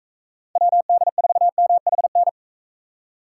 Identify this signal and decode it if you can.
Morse code, heard as a keyed tone, about 44 wpm, 700 Hz. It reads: WD4MHN